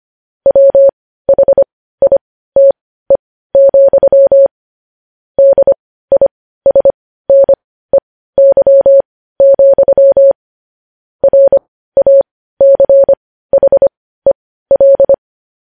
Morse code, heard as a keyed tone, about 25 wpm, spaced as for 15 wpm, 550 Hz. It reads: WHITE, DISNEY, RACHEL
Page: news-headlines-in-morse-code-at-15-wpm